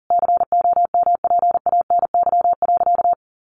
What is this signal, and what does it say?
Morse code, heard as a keyed tone, about 40 wpm, 700 Hz. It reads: COMPANY.
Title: Sentences from Top 1100 Words 40wpm